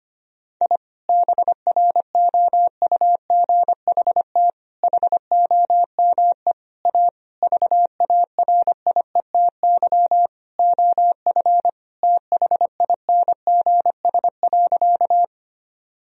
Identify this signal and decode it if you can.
Morse code, heard as a keyed tone, about 25 wpm, 700 Hz. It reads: I BROUGHT HOME A VARIETY OF THINGS.
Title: Sentences from Top 1000 Words - Review Part 4 - 25wpm